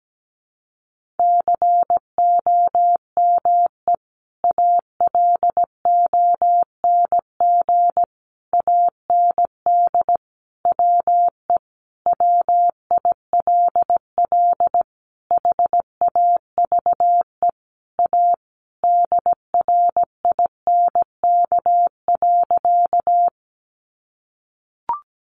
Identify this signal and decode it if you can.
Morse code, heard as a keyed tone, about 17 wpm, 700 Hz. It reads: COME ALONG AND WE WILL HAVE A DRINK.
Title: Sentences from Top 1500 Words 17wpm